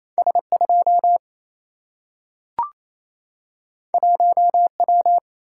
Morse code, heard as a keyed tone, about 28 wpm, 700 Hz. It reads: S 2 1 W
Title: Single-Letter Single-Number 28wpm